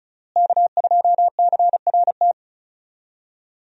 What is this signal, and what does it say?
Morse code, heard as a keyed tone, about 36 wpm, 700 Hz. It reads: K2CRT
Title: USA Call Signs - 5 Characters - Repeated 3x - 35wpm